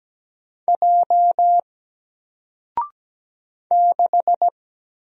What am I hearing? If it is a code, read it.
Morse code, heard as a keyed tone, about 17 wpm, 700 Hz. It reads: J 6
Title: Single Letter-Number Alt 17wpm